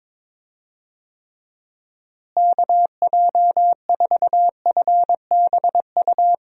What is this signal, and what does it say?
Morse code, heard as a keyed tone, about 22 wpm, 700 Hz. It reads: KJ4FBU
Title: USA Call Signs - 6 Characters - Repeated 2x - 22wpm